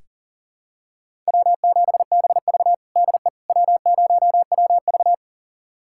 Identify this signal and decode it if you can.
Morse code, heard as a keyed tone, about 40 wpm, 700 Hz. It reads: W7BV DE W0WV
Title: USA Callsign DE Callsign - 4 Characters - 40wpm